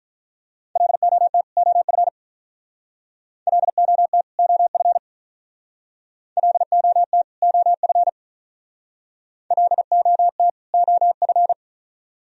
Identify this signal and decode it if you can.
Morse code, heard as a keyed tone, about 44 wpm, 700 Hz. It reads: LOT OF LOT OF LOT OF LOT OF